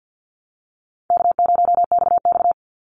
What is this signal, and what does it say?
Morse code, heard as a keyed tone, about 50 wpm, 700 Hz. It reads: K0XX
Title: Easy Call Signs Repeated 3x - 50wpm